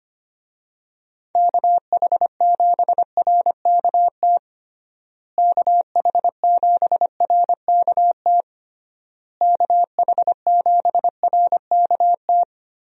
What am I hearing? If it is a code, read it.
Morse code, heard as a keyed tone, about 25 wpm, 700 Hz. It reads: KH7RKT KH7RKT KH7RKT
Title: USA Call Signs - 6 Characters - Repeated 3x - 25wpm